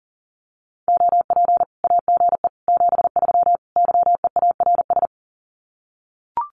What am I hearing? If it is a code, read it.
Morse code, heard as a keyed tone, about 37 wpm, 700 Hz. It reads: OP AGE 73 YEARS
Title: Common QSO Phrases 40wpm